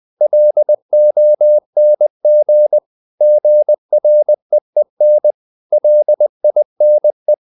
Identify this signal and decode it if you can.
Morse code, heard as a keyed tone, about 20 wpm, 600 Hz. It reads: LONG GREEN LINE
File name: the old man and the sea / the old man and the sea part 1 20wpm